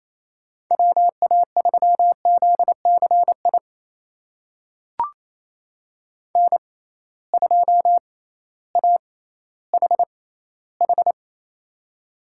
Morse code, heard as a keyed tone, about 28 wpm, spaced as for 15 wpm, 700 Hz. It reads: WA3ZCI N2AHH
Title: USA Call Signs - 6x Character Spacing - 28wpm